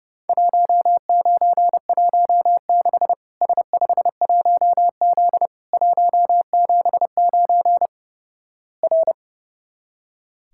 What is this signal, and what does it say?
Morse code, heard as a keyed tone, about 32 wpm, 700 Hz. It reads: 1916 S51Z 179 R
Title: Worked All Europe DX Contest - QTC Exchange - 30wpm